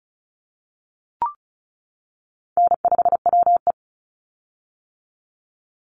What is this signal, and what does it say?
Morse code, heard as a keyed tone, about 34 wpm, 700 Hz. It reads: N5WE
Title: Call Signs - 35wpm